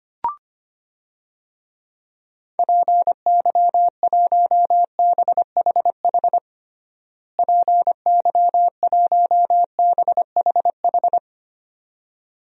Morse code, heard as a keyed tone, about 25 wpm, 700 Hz. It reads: PY1BHH PY1BHH
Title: DXCC-100 Call Signs - 6 Characters - Repeated 2x - 25wpm